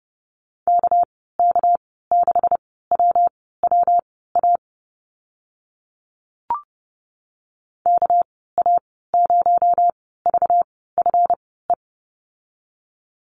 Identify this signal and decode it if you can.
Morse code, heard as a keyed tone, about 30 wpm, spaced as for 17 wpm, 700 Hz. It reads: KK6WWA KA0VFE
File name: Lesson 37g - Letter x - Call Signs 30wpm